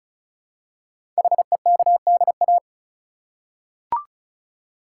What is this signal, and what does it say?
Morse code, heard as a keyed tone, about 36 wpm, 690 Hz. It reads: HEKDA
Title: Code Groups - 5 Letters - 35wpm